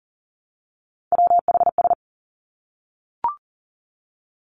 Morse code, heard as a keyed tone, about 42 wpm, 700 Hz. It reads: WHS
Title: Three Challenging Letter-Number Combinations 40wpm